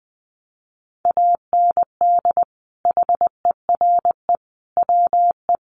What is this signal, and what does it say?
Morse code, heard as a keyed tone, about 20 wpm, 700 Hz. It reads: AND HERE WE
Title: A Sound Of Thunder 20wpm